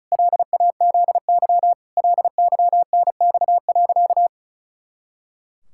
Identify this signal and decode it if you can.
Morse code, heard as a keyed tone, about 34 wpm, 700 Hz. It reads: LAZY LYNX.
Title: Pangram Challenge v2 - 35wpm